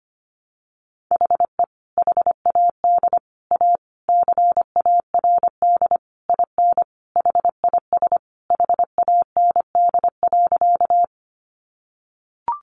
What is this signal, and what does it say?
Morse code, heard as a keyed tone, about 25 wpm, 700 Hz. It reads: HE HAD A CARD IN HIS HAND.
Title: Sentences from Top 600 Words 25wpm